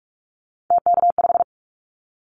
Morse code, heard as a keyed tone, about 44 wpm, 700 Hz. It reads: TK5
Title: Callsign Prefixes for Top 100 DXCC - 45wpm